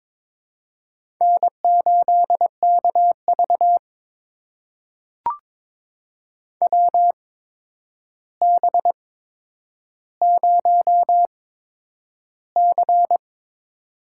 Morse code, heard as a keyed tone, about 22 wpm, spaced as for 9 wpm, 700 Hz. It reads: N8KV WB0C
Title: USA Call Signs - 4 Characters - 8x Character Spacing - 22wpm